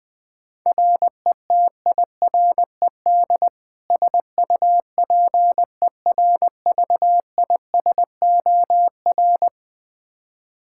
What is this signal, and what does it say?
Morse code, heard as a keyed tone, about 21 wpm, 700 Hz. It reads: RETIRED SUPERVISOR
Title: K1USN SST QSO Practice 20wpm